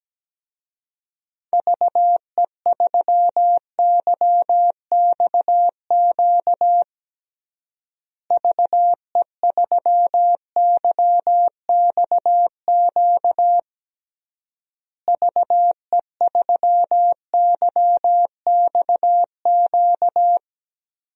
Morse code, heard as a keyed tone, about 17 wpm, 700 Hz. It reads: VE3YXQ VE3YXQ VE3YXQ
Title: DXCC-100 Call Signs - 6 Characters - Repeated 3x - 17wpm